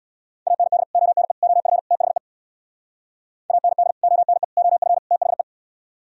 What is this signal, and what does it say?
Morse code, heard as a keyed tone, about 64 wpm, 700 Hz. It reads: AND ONE OF THE AND ONE OF THE